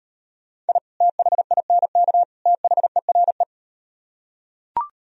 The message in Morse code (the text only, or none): I THINK THERE